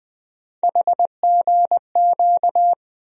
Morse code, heard as a keyed tone, about 21 wpm, 700 Hz. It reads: HGQ
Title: USA Call Sign Postfixes - 3 Letters 20wpm